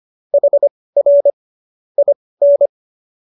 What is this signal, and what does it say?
Morse code, heard as a keyed tone, about 25 wpm, spaced as for 18 wpm, 570 Hz. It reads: HR IN
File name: qso201 18